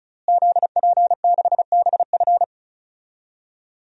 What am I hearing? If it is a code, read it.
Morse code, heard as a keyed tone, about 36 wpm, 700 Hz. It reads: ZP6BF
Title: DXCC-100 Call Signs - 5 Characters - Repeated 3x - 35wpm